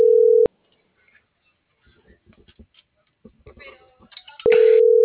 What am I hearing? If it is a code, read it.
Ringback tone (North America), 440 + 480 Hz.